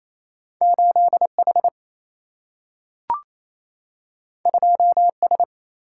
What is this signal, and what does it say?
Morse code, heard as a keyed tone, about 28 wpm, 700 Hz. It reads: 8H 2S
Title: Single-Letter Single-Number 28wpm